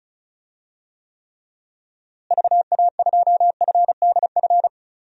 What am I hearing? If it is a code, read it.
Morse code, heard as a keyed tone, about 36 wpm, 700 Hz. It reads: VA2FDF
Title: DXCC-100 Call Signs - 6 Characters - Repeated 2x - 35wpm